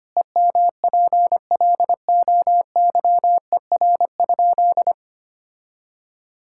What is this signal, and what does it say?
Morse code, heard as a keyed tone, about 25 wpm, 700 Hz. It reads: EMPLOYER?
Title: Sentences from Top 1600 Words 25wpm